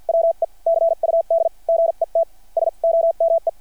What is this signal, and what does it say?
Morse code, heard as a keyed tone, about 50 wpm, spaced as for 42 wpm, 660 Hz. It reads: WE CUD GET SOME
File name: archived / CRSnet